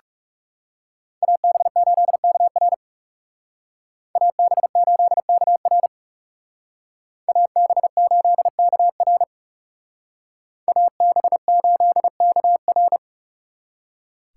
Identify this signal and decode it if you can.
Morse code, heard as a keyed tone, about 38 wpm, 700 Hz. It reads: AB8KR AB8KR AB8KR AB8KR